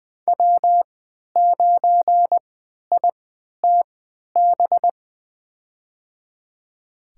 Morse code, heard as a keyed tone, about 20 wpm, spaced as for 11 wpm, 700 Hz. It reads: W9ITB
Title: USA Call Signs - 5 Characters - 3x Character Spacing - 20wpm